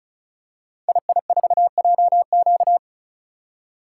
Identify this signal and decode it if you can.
Morse code, heard as a keyed tone, about 36 wpm, 700 Hz. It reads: II4JQ
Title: Code Groups - 5 Letter:Numbers - 35wpm